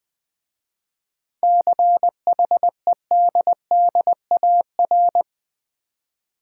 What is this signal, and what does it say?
Morse code, heard as a keyed tone, about 20 wpm, 700 Hz. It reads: CHEDDAR
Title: Easy Words with Hints - 20wpm